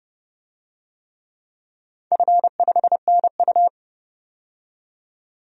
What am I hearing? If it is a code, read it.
Morse code, heard as a keyed tone, about 30 wpm, 700 Hz. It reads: F5NU
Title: DXCC-100 Call Signs - 4 Characters - 30wpm